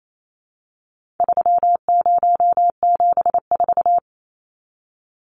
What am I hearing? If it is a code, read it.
Morse code, heard as a keyed tone, about 28 wpm, 700 Hz. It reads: 3074
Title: Numbers - 4-Digit 28wpm